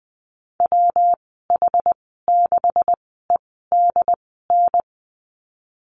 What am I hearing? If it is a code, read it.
Morse code, heard as a keyed tone, about 20 wpm, spaced as for 14 wpm, 700 Hz. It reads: WH6EDN
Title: 161 Lesson 26g - Letter g - Call Signs 20wpm